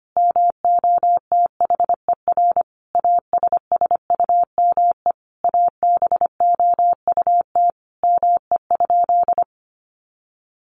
Morse code, heard as a keyed tone, about 25 wpm, 700 Hz. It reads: MOTHER ASSUME ABOUT ME?
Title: Sentences from Top 1500 Words 25wpm